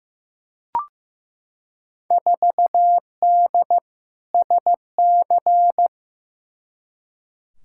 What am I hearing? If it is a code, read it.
Morse code, heard as a keyed tone, about 16 wpm, 700 Hz. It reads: 4D SC
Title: Field Day Exchanges Encore - 15wpm